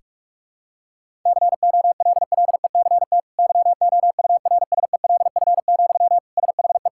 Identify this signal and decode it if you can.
Morse code, heard as a keyed tone, about 46 wpm, 700 Hz. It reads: COLLECT YOURSELF, SHE
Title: Sentences from Top 1600 Words 45wpm